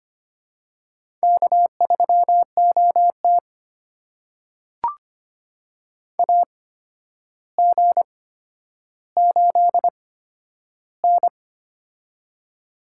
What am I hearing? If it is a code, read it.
Morse code, heard as a keyed tone, about 25 wpm, spaced as for 11 wpm, 700 Hz. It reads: K3OT AG8N
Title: USA Call Signs - 4 Characters - 8x Character Spacing - 25wpm